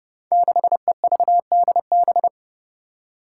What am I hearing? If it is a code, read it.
Morse code, heard as a keyed tone, about 30 wpm, 700 Hz. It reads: 6EVDB
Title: Five Challenging Letter-Number Combinations 30wpm